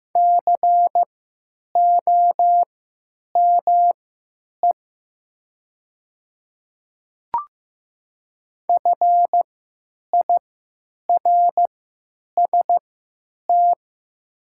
Morse code, heard as a keyed tone, about 15 wpm, spaced as for 9 wpm, 700 Hz. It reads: COME FIRST
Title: 3x Character Spacing - Top 200 Words - 15wpm